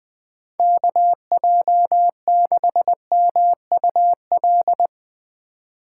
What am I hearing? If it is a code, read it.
Morse code, heard as a keyed tone, about 22 wpm, 700 Hz. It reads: KJ6MUL
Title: Speed Racing - USA 2x3 Call Signs 20wpm